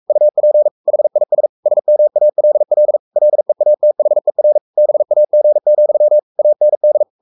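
Morse code, heard as a keyed tone, about 44 wpm, 600 Hz. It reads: UP HIS SMALL LEATHER BAG, AND
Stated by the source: Morse code at 43 wpm sending Ken Follett's novel Never in English, from CW med Gnister.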